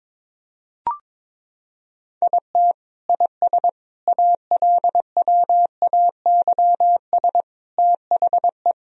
Morse code, heard as a keyed tone, about 21 wpm, 700 Hz. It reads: IT IS ALWAYS THE
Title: Sentences from Top 1000 Words - Review Part 1 - 22wpm